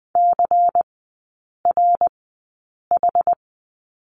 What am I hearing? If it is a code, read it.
Morse code, heard as a keyed tone, about 20 wpm, spaced as for 8 wpm, 700 Hz. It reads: CRH
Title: 121 Lesson 20e - Number 3 - Three Characters 20wpm